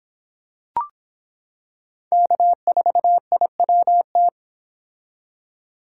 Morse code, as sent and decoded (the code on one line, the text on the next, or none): -.- ....- .. .-- -
K4IWT